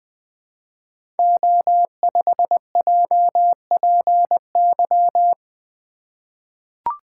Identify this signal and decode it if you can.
Morse code, heard as a keyed tone, about 20 wpm, 700 Hz. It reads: O5JPY
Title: Code Groups - 5 Letter:Numbers - 20wpm